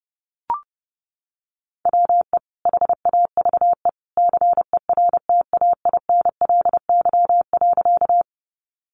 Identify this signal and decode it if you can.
Morse code, heard as a keyed tone, about 30 wpm, 700 Hz. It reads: WE HAVE CERTAINLY.
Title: Sentences from Top 900 Words 30wpm